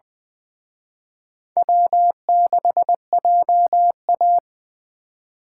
Morse code, heard as a keyed tone, about 23 wpm, 700 Hz. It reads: W6JA
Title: Speed Racing - USA 1x2 Call Signs 20wpm